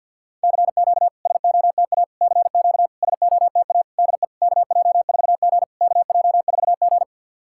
Morse code, heard as a keyed tone, about 52 wpm, 690 Hz. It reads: CQ SOTA CQ SOTA DE KJ4G KJ4G
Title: SOTA QSO Practice 50wpm